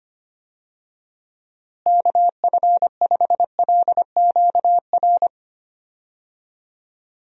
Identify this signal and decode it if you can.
Morse code, heard as a keyed tone, about 25 wpm, 700 Hz. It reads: KF5LQR
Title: USA Call Signs - 6 Characters - Repeated 3x - 25wpm